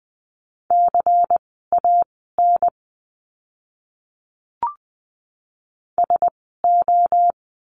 Morse code, heard as a keyed tone, about 20 wpm, spaced as for 14 wpm, 700 Hz. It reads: CAN SO
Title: 230 Lesson 37f - Letter x - Words 20wpm